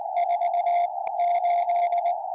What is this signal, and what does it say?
Morse code, heard as a keyed tone, about 20 wpm, 700 Hz, with radio noise. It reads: 4 8
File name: tropo / g4rqi 70cm